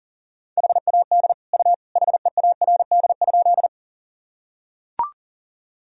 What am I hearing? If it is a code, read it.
Morse code, heard as a keyed tone, about 40 wpm, 700 Hz. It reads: HAD U HEARD?
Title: Sentences from Top 400 Words and Abbreviations 40wpm